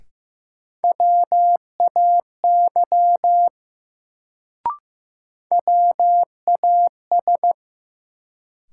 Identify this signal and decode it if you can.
Morse code, heard as a keyed tone, about 15 wpm, 700 Hz. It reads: WAY WAS
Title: Sets of 2 Words - New Word Drill - 15wpm